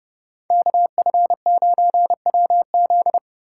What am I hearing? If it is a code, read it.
Morse code, heard as a keyed tone, about 32 wpm, 700 Hz. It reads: KF9WZ
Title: USA Call Signs - 5 Characters - 3x Character Spacing - 30wpm